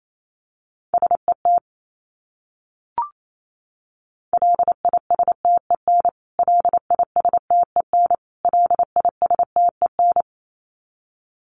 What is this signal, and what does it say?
Morse code, heard as a keyed tone, about 28 wpm, 700 Hz. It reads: SET LISTEN LISTEN LISTEN
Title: Top 500 Words Repeated 28wpm